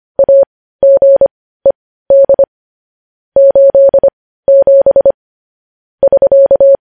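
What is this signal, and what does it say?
Morse code, heard as a keyed tone, about 25 wpm, spaced as for 15 wpm, 550 Hz. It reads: AGED 87 <SK>
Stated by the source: Morse code news headlines sent at 15, preceded by vvv.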